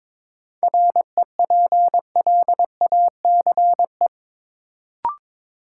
Morse code, heard as a keyed tone, about 22 wpm, 700 Hz. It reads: REPLACE
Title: Sentences from Top 1400 Words - New Word Drill - 22wpm